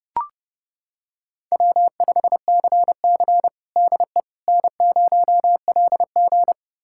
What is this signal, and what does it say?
Morse code, heard as a keyed tone, about 30 wpm, 700 Hz. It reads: W5CC DE N0LG
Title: USA Callsign DE Callsign - 4 Characters - 30wpm